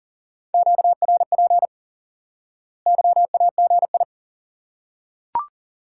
Morse code, heard as a keyed tone, about 37 wpm, 700 Hz. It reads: QRP YAGI
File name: IQE Course 3 - Lesson 89 - YAGI - Two Elements - 40wpm